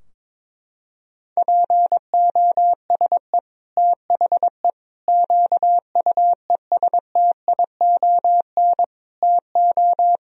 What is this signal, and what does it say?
Morse code, heard as a keyed tone, about 21 wpm, 700 Hz. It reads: POSE THE QUESTION TO
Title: Sentences from Top 1000 Words - Review Part 2 - 22wpm